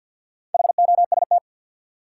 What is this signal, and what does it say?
Morse code, heard as a keyed tone, about 53 wpm, 700 Hz. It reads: HOST
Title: Four Letter Words - 50wpm